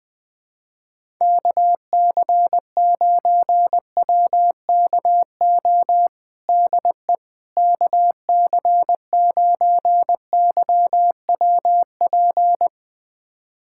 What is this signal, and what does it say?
Morse code, heard as a keyed tone, about 20 wpm, 700 Hz. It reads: KC9WKO DE KC9YWP
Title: USA Callsign DE Callsign 20wpm